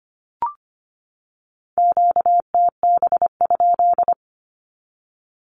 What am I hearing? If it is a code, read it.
Morse code, heard as a keyed tone, about 25 wpm, 700 Hz. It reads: QTB?
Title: Q-Codes 25wpm